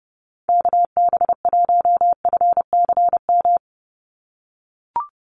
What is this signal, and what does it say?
Morse code, heard as a keyed tone, about 30 wpm, 700 Hz. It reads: KB1FCM
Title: USA Call Signs 30wpm